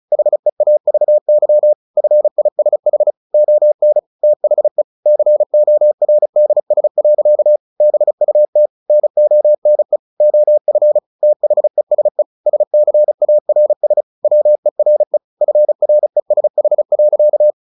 Morse code, heard as a keyed tone, about 34 wpm, 600 Hz. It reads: HEAVY FISH ON THE CORDS. BUT NONE OF THESE SCARS WERE FRESH.
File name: the old man and the sea / the old man and the sea 1 35wpm